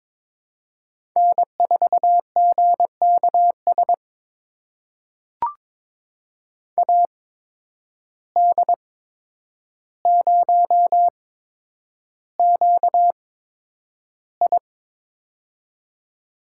Morse code, heard as a keyed tone, about 22 wpm, spaced as for 9 wpm, 700 Hz. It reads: N4GKS AD0QI
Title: USA Call Signs - 5 Characters - 8x Character Spacing - 22wpm